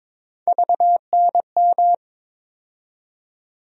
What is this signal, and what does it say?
Morse code, heard as a keyed tone, about 21 wpm, 700 Hz. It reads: VNM
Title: USA Call Sign Postfixes - 3 Letters 22wpm